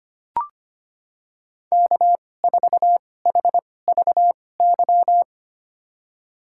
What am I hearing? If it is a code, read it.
Morse code, heard as a keyed tone, about 25 wpm, spaced as for 18 wpm, 700 Hz. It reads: K4HVY
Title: USA Call Signs - 5 Characters - 2x Character Spacing - 25wpm